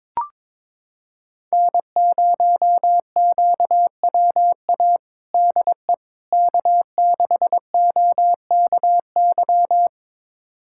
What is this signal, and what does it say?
Morse code, heard as a keyed tone, about 22 wpm, 700 Hz. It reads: N0QWA DE K6OKY
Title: USA Callsign DE Callsign - 5 Characters - 22wpm